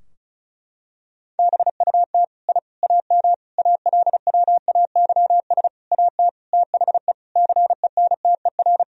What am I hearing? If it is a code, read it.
Morse code, heard as a keyed tone, about 36 wpm, 700 Hz. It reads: BUT I AM ALWAYS AT THE CENTER
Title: Sentences from Top 1000 Words - Review Part 2 - 35wpm